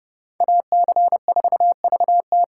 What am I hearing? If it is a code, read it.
Morse code, heard as a keyed tone, about 31 wpm, 700 Hz. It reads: AC4VT